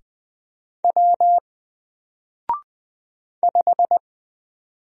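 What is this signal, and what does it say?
Morse code, heard as a keyed tone, about 20 wpm, 700 Hz. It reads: W 5
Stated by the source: Single letter and number Morse drill at 20 wpm, from N4LCD.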